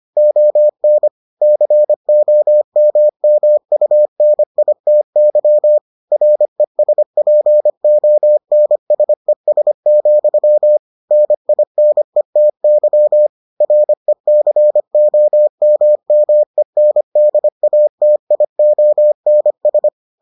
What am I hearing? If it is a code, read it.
Morse code, heard as a keyed tone, about 25 wpm, 600 Hz. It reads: ON COMMUNITY RESPONSES, NINETY RECOMMENDATIONS